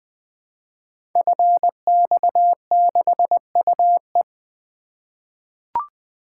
Morse code, heard as a keyed tone, about 20 wpm, 700 Hz. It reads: FX6UE
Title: Code Groups - 5 Letter:Numbers - 20wpm